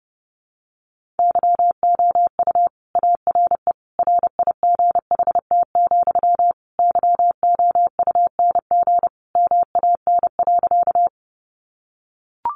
Morse code, heard as a keyed tone, about 30 wpm, 700 Hz. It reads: YOU ARE RIGHT, YOUNG MAN.
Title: Sentences from Top 400 Words 30wpm